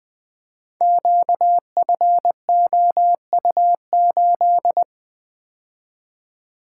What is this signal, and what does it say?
Morse code, heard as a keyed tone, about 20 wpm, 700 Hz. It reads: QFOU8
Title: Code Groups - 5 Letter:Numbers - 20wpm